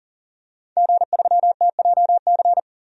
Morse code, heard as a keyed tone, about 37 wpm, 700 Hz. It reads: G3TJC
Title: DXCC-100 Call Signs - 5 Characters - 40wpm